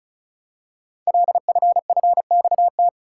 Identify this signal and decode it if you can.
Morse code, heard as a keyed tone, about 36 wpm, 690 Hz. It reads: LFFXT
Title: Code Groups - 5 Letters - 35wpm